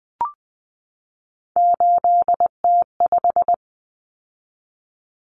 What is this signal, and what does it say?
Morse code, heard as a keyed tone, about 20 wpm, 700 Hz. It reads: 8T5
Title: Common Cut Numbers - 3-Digit 20wpm